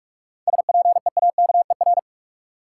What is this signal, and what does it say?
Morse code, heard as a keyed tone, about 46 wpm, 700 Hz. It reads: SPEAKER
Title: Seven Letter Words - 45wpm